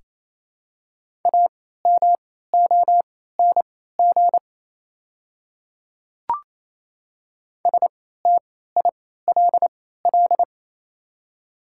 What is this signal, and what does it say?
Morse code, heard as a keyed tone, about 27 wpm, spaced as for 16 wpm, 700 Hz. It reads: AMONG STILL
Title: 3x Character Spacing - Top 500 Words - 28wpm